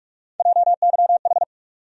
Morse code, heard as a keyed tone, about 43 wpm, 700 Hz. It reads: JYH
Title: USA Call Sign Postfixes - 3 Letters 45wpm